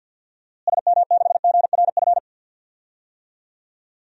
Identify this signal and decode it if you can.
Morse code, heard as a keyed tone, about 53 wpm, 700 Hz. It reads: SM6GRF
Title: DXCC-100 Call Signs - 6 Characters - Repeated 2x - 50wpm